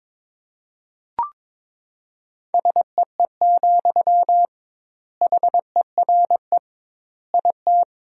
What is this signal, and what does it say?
Morse code, heard as a keyed tone, about 22 wpm, 700 Hz. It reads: SEE, HERE IT
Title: 2x Word-Spacing - Sentences from Top 200 Words - 22wpm